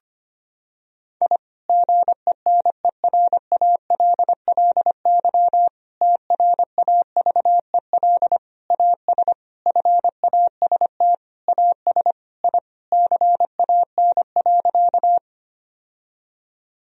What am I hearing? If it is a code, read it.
Morse code, heard as a keyed tone, about 25 wpm, 700 Hz. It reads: I GENERALLY TRAVEL AS FAST AS I CAN.
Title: Sentences from Top 1000 Words - Review Part 4 - 25wpm